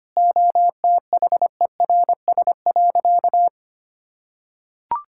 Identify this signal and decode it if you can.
Morse code, heard as a keyed tone, about 25 wpm, 700 Hz. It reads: OTHERS.
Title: Sentences from Top 1000 Words - Review Part 6 - 25wpm